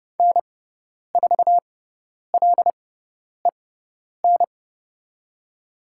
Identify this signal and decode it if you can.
Morse code, heard as a keyed tone, about 30 wpm, spaced as for 10 wpm, 700 Hz. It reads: N4LEN
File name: Lesson 16g - Letter c - Call Signs 30wpm